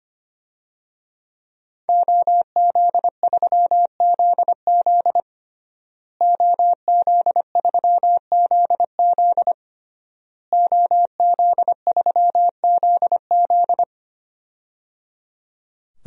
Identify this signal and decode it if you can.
Morse code, heard as a keyed tone, about 25 wpm, 700 Hz. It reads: OZ3ZZ OZ3ZZ OZ3ZZ
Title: DXCC-100 Call Signs - 5 Characters - Repeated 3x - 25wpm